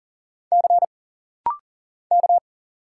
Morse code, heard as a keyed tone, about 39 wpm, 700 Hz. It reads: C K
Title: Single Letter-Number Alt - Rapid-Fire 40wpm